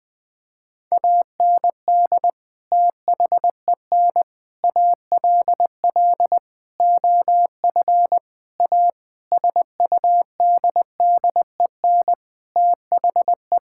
Morse code, heard as a keyed tone, about 20 wpm, 700 Hz. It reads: AND THEN ALL OF A SUDDEN THE
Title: Sentences with 6-Word Phrases - 20wpm